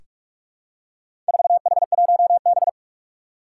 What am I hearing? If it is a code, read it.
Morse code, heard as a keyed tone, about 47 wpm, 700 Hz. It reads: 4H1B
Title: Four Challenging Letter-Number Combinations 45wpm